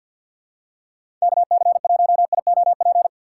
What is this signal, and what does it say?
Morse code, heard as a keyed tone, about 53 wpm, 700 Hz. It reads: KC1IOP
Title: USA Call Signs - 6 Characters - 3x Character Spacing - 50wpm